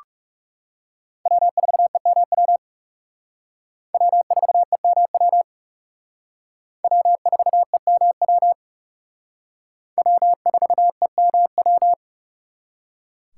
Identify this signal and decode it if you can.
Morse code, heard as a keyed tone, about 37 wpm, 700 Hz. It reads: W4EMW W4EMW W4EMW W4EMW
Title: Speed Racing - USA 1x3 Call Signs 30wpm